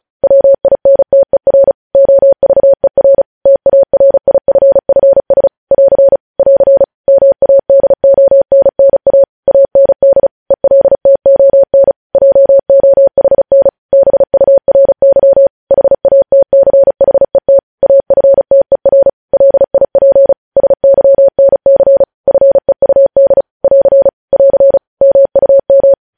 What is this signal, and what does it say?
Morse code, heard as a keyed tone, about 36 wpm, 550 Hz. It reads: WINTER OVER TARIFFS <AR> <AR> MADONNA AND ELTON JOHN BURY HATCHET AFTER LIP SYNC FEUD <AR> <AR> MUM
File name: podcast / News headlines-1x-35x35